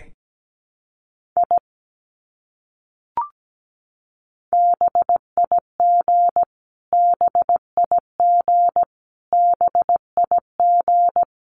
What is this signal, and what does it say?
Morse code, heard as a keyed tone, about 17 wpm, 700 Hz. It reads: I BIG BIG BIG
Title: Top 300 Words Repeated 17wpm